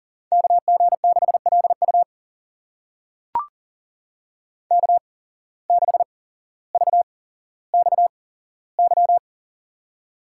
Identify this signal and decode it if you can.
Morse code, heard as a keyed tone, about 40 wpm, spaced as for 17 wpm, 700 Hz. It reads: KG6LU K6VXY
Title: USA Call Signs - 5 Characters - 8x Character Spacing - 40wpm